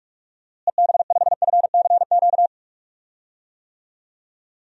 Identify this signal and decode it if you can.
Morse code, heard as a keyed tone, about 46 wpm, 700 Hz. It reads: EB5FCQ
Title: DXCC-100 Call Signs - 6 Characters - Repeated 2x - 45wpm